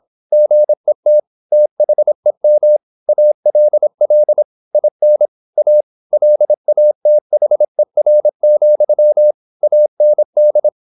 Morse code, heard as a keyed tone, about 26 wpm, 600 Hz. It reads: GET THEM ALL IN A LATHER, AND
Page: Never 26wpm | CW med Gnister